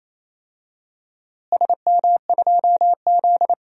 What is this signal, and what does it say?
Morse code, heard as a keyed tone, about 28 wpm, 700 Hz. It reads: SM2Z
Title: DXCC-100 Call Signs - 4 Characters - Repeated 3x - 28wpm